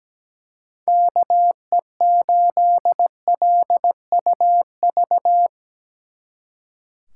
Morse code, heard as a keyed tone, about 19 wpm, 700 Hz. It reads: KE8LUV
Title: Speed Racing - USA 2x3 Call Signs 17wpm